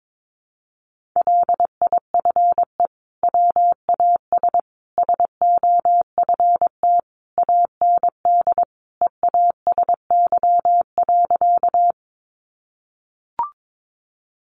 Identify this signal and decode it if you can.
Morse code, heard as a keyed tone, about 22 wpm, 700 Hz. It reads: LIFE WAS SOFT AND EASY.
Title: Sentences from Top 1500 Words 22wpm